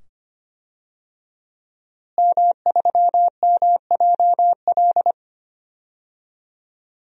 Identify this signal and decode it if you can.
Morse code, heard as a keyed tone, about 25 wpm, 700 Hz. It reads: M3MJL
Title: DXCC-100 Call Signs - 5 Characters - Repeated 2x - 25wpm